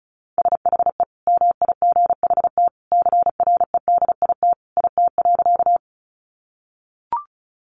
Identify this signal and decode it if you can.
Morse code, heard as a keyed tone, about 34 wpm, 700 Hz. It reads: SHE MIGHT CREDIT IT.
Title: Sentences from Top 1000 Words 35wpm